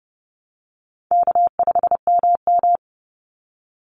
Morse code, heard as a keyed tone, about 30 wpm, 700 Hz. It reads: K5MM
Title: Call Signs Repeated 2x - 30wpm